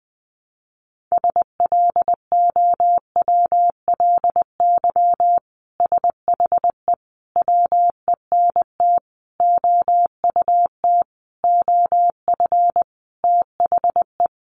Morse code, heard as a keyed tone, about 20 wpm, 700 Hz. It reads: SLOWLY SHE WENT OUT OF THE
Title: Sentences from Top 1100 Words 20wpm